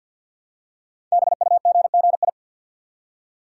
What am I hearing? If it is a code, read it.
Morse code, heard as a keyed tone, about 52 wpm, 700 Hz. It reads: BUGGI
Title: Code Groups - 5 Letter:Numbers - 50wpm